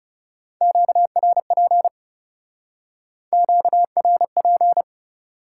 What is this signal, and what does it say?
Morse code, heard as a keyed tone, about 34 wpm, 700 Hz. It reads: QRP QRP
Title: Speed-Racing Top 90 QSO Elements 30wpm